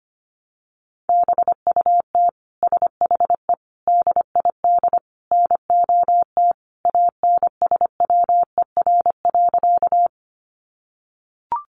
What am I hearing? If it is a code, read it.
Morse code, heard as a keyed tone, about 25 wpm, 700 Hz. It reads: BUT SHE DID NOT ANSWER.
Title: Sentences from Top 600 Words 25wpm